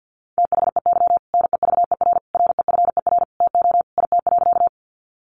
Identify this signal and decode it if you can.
Morse code, heard as a keyed tone, about 50 wpm, 700 Hz. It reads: THEY NEVER REFER TO IT.
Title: Sentences from Top 1000 Words 50wpm